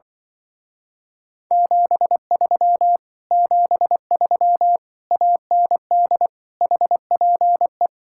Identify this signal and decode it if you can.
Morse code, heard as a keyed tone, about 24 wpm, 700 Hz. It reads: 73 73 AND HPE
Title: Speed-Racing Common QSO Phrases - 20wpm